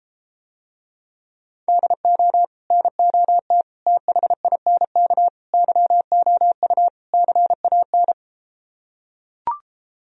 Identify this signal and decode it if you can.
Morse code, heard as a keyed tone, about 33 wpm, 700 Hz. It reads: DO NOT THINK YOU CAN